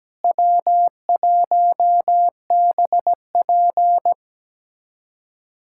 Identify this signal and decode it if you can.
Morse code, heard as a keyed tone, about 17 wpm, 700 Hz. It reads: W1BP
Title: USA Callsign DE Callsign - 4 Characters - 17wpm